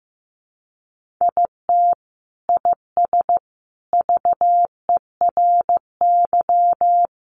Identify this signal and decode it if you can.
Morse code, heard as a keyed tone, about 15 wpm, 700 Hz. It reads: IT IS VERY
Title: Sentences from Top 1400 Words 15wpm